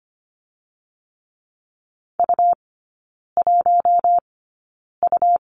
Morse code, heard as a keyed tone, about 25 wpm, spaced as for 9 wpm, 700 Hz. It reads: U1U
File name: course / 25:10wpm / 100 Lesson 17e - Letter u - Three Characters 25wpm